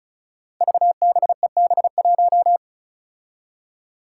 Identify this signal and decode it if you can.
Morse code, heard as a keyed tone, about 36 wpm, 700 Hz. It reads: VBEB1
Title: Five Challenging Letter-Number Combinations 35wpm